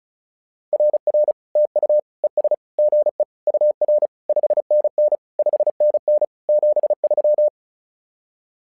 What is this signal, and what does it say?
Morse code, heard as a keyed tone, about 34 wpm, 600 Hz. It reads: RR TU ES GE UR 5NN 5NN 73
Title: SOTA QSO Practice 35wpm